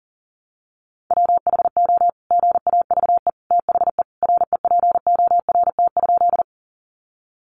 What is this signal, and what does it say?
Morse code, heard as a keyed tone, about 40 wpm, 700 Hz. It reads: WHO GAVE THE REPORT?
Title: Sentences from Top 900 Words 40wpm